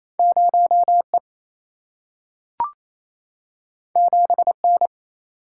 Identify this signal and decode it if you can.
Morse code, heard as a keyed tone, about 28 wpm, 700 Hz. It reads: 0 E 7 N
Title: Single-Letter Single-Number 28wpm